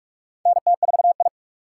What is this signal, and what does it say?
Morse code, heard as a keyed tone, about 45 wpm, 700 Hz. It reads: NT4I
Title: USA Call Signs - 4 Characters - 4x Character Spacing - 45wpm